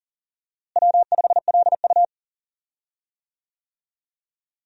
Morse code, heard as a keyed tone, about 40 wpm, 700 Hz. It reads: W5LU
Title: USA Call Signs - 4 Characters - Repeated 3x - 40wpm